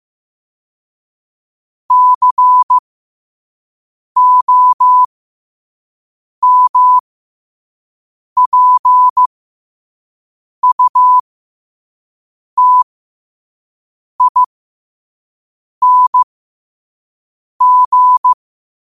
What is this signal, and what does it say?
Morse code, heard as a keyed tone, about 15 wpm, spaced as for 5 wpm, 1000 Hz. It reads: COMPUTING
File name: qotdfiles / 5wpm